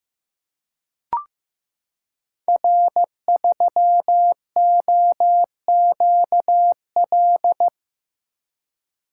Morse code, heard as a keyed tone, about 15 wpm, 700 Hz. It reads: R3OQL
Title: Code Groups - 5 Letter:Numbers - 15wpm